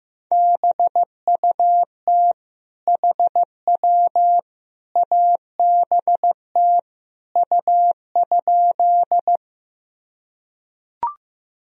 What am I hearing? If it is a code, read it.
Morse code, heard as a keyed tone, about 15 wpm, 700 Hz. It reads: BUT HW ABT U?
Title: Sentences from Top 100 Words and Abbreviations 15wpm